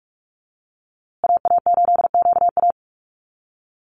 Morse code, heard as a keyed tone, about 44 wpm, 700 Hz. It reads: AA8QA
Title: USA Call Signs Repeated 3x 45wpm